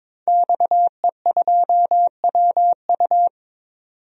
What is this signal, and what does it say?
Morse code, heard as a keyed tone, about 22 wpm, 700 Hz. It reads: XE2WU
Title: DXCC-100 Call Signs - 5 Characters - 22wpm